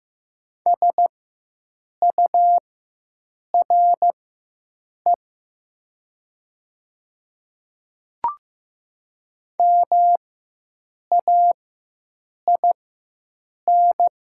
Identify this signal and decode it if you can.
Morse code, heard as a keyed tone, about 15 wpm, spaced as for 7 wpm, 700 Hz. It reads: SURE MAIN
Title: 4x Character Spacing - Top 500 Words - 15wpm